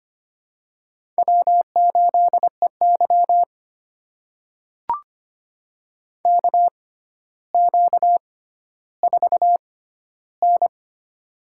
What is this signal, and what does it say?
Morse code, heard as a keyed tone, about 25 wpm, spaced as for 13 wpm, 700 Hz. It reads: W8EY KQ4N
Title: USA Call Signs - 4 Characters - 6x Character Spacing - 25wpm